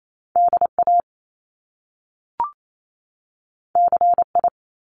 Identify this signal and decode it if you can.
Morse code, heard as a keyed tone, about 28 wpm, 700 Hz. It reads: DA CI
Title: Top Two Letter Combinations in Words 28wpm